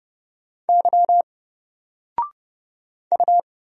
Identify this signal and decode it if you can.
Morse code, heard as a keyed tone, about 30 wpm, 700 Hz. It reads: Y U
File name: Single-Letter-Number-30wpm